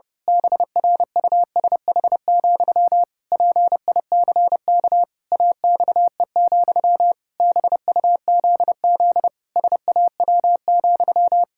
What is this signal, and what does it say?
Morse code, heard as a keyed tone, about 30 wpm, 700 Hz. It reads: BRUSH, PICK AXE, BUZZ SAW,
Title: Pangram Challenge v2 - 30wpm